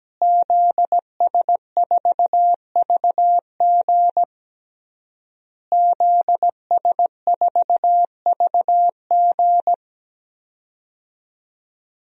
Morse code, heard as a keyed tone, about 17 wpm, 700 Hz. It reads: ZS4VG ZS4VG
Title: DXCC-100 Call Signs - 5 Characters - Repeated 3x - 17wpm